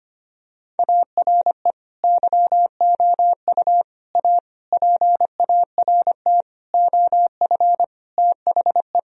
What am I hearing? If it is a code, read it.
Morse code, heard as a keyed tone, about 25 wpm, 700 Hz. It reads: ARE YOU A PART OF THE
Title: Sentences from Top 1000 Words - Review Part 4 - 25wpm